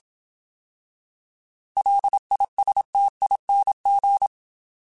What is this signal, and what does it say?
Morse code, heard as a keyed tone, about 27 wpm, 800 Hz. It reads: LISTING